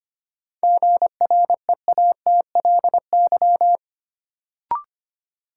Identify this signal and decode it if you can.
Morse code, heard as a keyed tone, about 25 wpm, 700 Hz. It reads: GREATLY
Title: Sentences from Top 1400 Words - New Word Drill - 25wpm